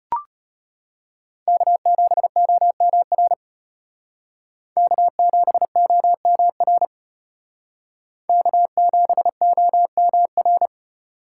Morse code, heard as a keyed tone, about 31 wpm, 700 Hz. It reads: K7OMR K7OMR K7OMR
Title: Speed Racing - USA 1x3 Call Signs 25wpm